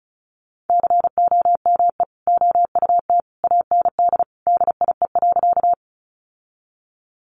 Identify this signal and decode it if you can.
Morse code, heard as a keyed tone, about 34 wpm, 700 Hz. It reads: COME OUT AND DIE.
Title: Sentences from Top 1300 Words 35wpm